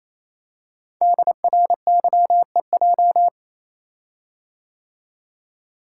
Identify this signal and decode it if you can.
Morse code, heard as a keyed tone, about 29 wpm, 700 Hz. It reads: DRYEJ
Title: Code Groups - 5 Letters - 28wpm